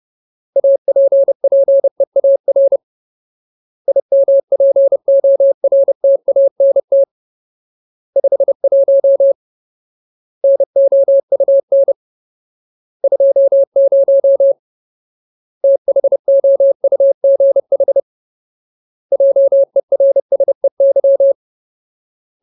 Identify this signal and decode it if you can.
Morse code, heard as a keyed tone, about 30 wpm, 550 Hz. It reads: APPEAR IMPORTANT 51 NOUN 20 THOUGH JERSEY
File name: Ultimate Word List / 30WPM Ultimate